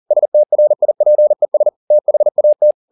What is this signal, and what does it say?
Morse code, heard as a keyed tone, about 40 wpm, 600 Hz. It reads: STRIPES THAT
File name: the old man and the sea 2 40wpm